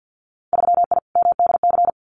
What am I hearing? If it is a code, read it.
Morse code, heard as a keyed tone, about 50 wpm, 700 Hz. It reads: 3I MDC
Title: Winter Field Day Exchanges Encore 50wpm